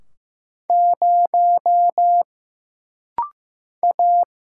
Morse code, heard as a keyed tone, about 15 wpm, 700 Hz. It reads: 0 A
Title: Single Letter-Number Alt - Rapid-Fire 15wpm